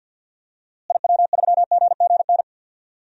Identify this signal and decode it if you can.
Morse code, heard as a keyed tone, about 50 wpm, 700 Hz. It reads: IW3GGN
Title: DXCC-100 Call Signs - 6 Characters - Repeated 2x - 50wpm